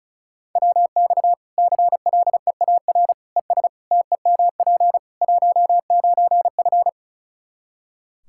Morse code, heard as a keyed tone, about 34 wpm, 700 Hz. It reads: WX CLEAR ES TEMP 19F
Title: Practice QSO - Exchange 2 - Info - 35wpm